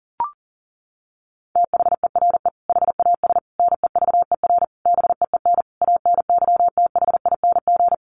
Morse code, heard as a keyed tone, about 37 wpm, 700 Hz. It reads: THERE HAS NEVER BEEN ANYTHING
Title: Sentences from Top 1500 Words 40wpm